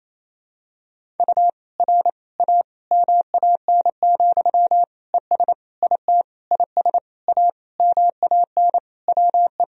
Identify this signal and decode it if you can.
Morse code, heard as a keyed tone, about 28 wpm, 700 Hz. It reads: U R A MAN, ES IT IS A MAN WE
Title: Sentences from Top 200 Words and Abbreviations 28wpm